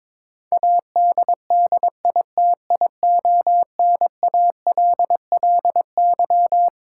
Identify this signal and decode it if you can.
Morse code, heard as a keyed tone, about 22 wpm, 700 Hz. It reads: ADDITIONALLY
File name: Twelve Letter Words - 22wpm